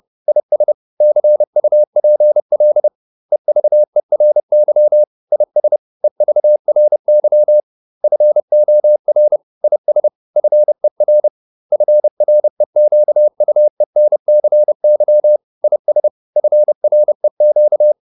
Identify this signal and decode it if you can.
Morse code, heard as a keyed tone, about 30 wpm, 600 Hz. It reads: IS CUPL EVERY IS EVRY FOR IS FER FREQUENCY IS FREQ